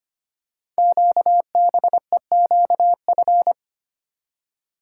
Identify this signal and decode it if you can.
Morse code, heard as a keyed tone, about 25 wpm, 700 Hz. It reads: QBEQF
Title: Code Groups - 5 Letters - 25wpm